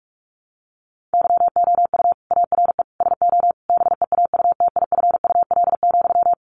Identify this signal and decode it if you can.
Morse code, heard as a keyed tone, about 44 wpm, 700 Hz. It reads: YOU ARE SO BEAUTIFUL,
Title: Sentences from Top 1200 Words 45wpm